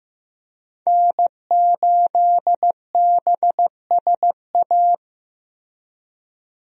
Morse code, heard as a keyed tone, about 16 wpm, 700 Hz. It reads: N8BSA
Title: USA Call Signs - 5 Characters - Repeated 3x - 15wpm